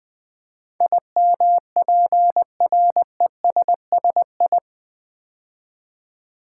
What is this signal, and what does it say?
Morse code, heard as a keyed tone, about 20 wpm, 700 Hz. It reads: IMPRESSI
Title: Build A Word - 4 Letters At A Time - 10-Letter Words - 20wpm